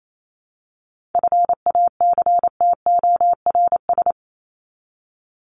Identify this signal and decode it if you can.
Morse code, heard as a keyed tone, about 28 wpm, 700 Hz. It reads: FACTORS
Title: ES:S Suffix - Top 100 Words with Base Words 28wpm